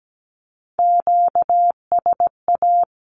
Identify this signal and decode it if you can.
Morse code, heard as a keyed tone, about 17 wpm, 700 Hz. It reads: QSA
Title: Q-Codes 17wpm